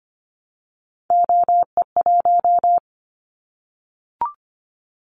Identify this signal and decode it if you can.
Morse code, heard as a keyed tone, about 25 wpm, 700 Hz. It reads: OE1
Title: Callsign Prefixes for Top 25 DXCC - 25wpm